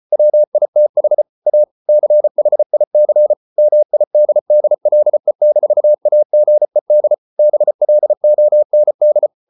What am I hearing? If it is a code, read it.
Morse code, heard as a keyed tone, about 34 wpm, 600 Hz. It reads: WITH A CHIC MIDDLE-AGED BLOND